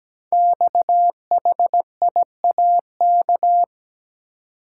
Morse code, heard as a keyed tone, about 17 wpm, 700 Hz. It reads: XHIAK
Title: Code Groups - 5 Letter:Numbers - 17wpm